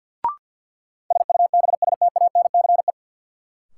Morse code, heard as a keyed tone, about 52 wpm, 700 Hz. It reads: SUBSTANCE
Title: Sentences from Top 1500 Words - New Word Drill - 50wpm